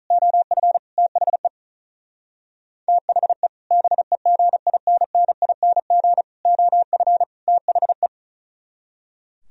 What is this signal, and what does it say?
Morse code, heard as a keyed tone, about 46 wpm, 700 Hz. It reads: OF THE THE BEGINNING OF THE